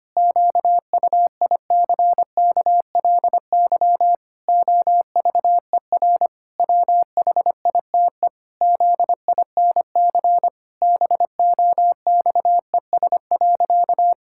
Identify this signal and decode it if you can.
Morse code, heard as a keyed tone, about 25 wpm, 700 Hz. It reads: QUICKLY OVER WHITE ZINC BOXES.
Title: Pangram Challenge v2 - 25wpm